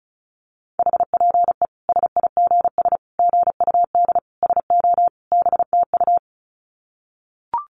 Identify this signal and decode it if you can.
Morse code, heard as a keyed tone, about 34 wpm, 700 Hz. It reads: HPE SIGS GUD SO BTU
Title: Common QSO Phrases 35wpm